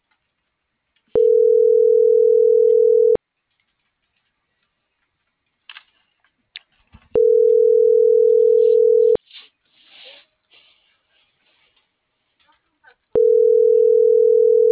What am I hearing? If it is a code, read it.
Ringback tone (North America), 440 + 480 Hz.